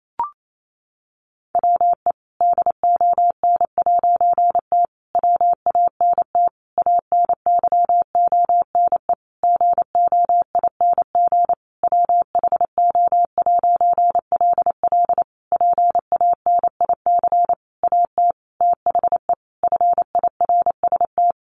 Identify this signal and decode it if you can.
Morse code, heard as a keyed tone, about 28 wpm, 700 Hz. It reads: WE DON'T WANT ANYONE GOING WHO'LL PANIC AT THE FIRST
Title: A Sound Of Thunder 28wpm